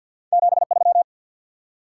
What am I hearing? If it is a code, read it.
Morse code, heard as a keyed tone, about 54 wpm, 700 Hz. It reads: 73
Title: Top 180 QSO Elements 50wpm